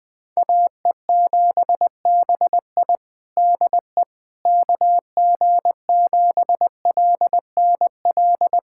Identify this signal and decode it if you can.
Morse code, heard as a keyed tone, about 21 wpm, 700 Hz. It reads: AE7BI DE KG7LNL
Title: USA Callsign DE Callsign 20wpm